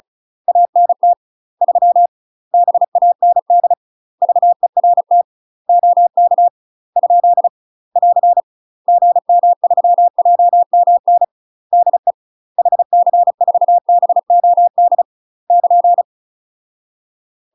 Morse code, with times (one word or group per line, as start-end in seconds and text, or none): 0.5-1.1 ANT
1.6-2.1 3
2.5-3.7 BAND
4.2-5.2 VERT
5.7-6.5 OK
7.0-7.5 ?
8.0-8.4 <AR>
8.9-11.2 GM3JMN
11.7-12.1 DE
12.6-15.0 HC4BOD
15.5-16.0 <KN>